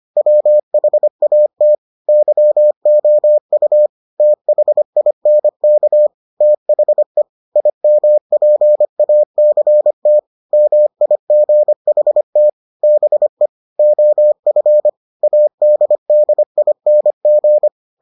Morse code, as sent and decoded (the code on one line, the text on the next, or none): .-- .... .- - / -.-- --- ..- / - .... .. -. -.- / - .... / .. -- .--. .- -.-. - / -- .. --. .... - / -... / --- ..-. / .- -.. -.. .. -. --.
WHAT YOU THINK THE IMPACT MIGHT BE OF ADDING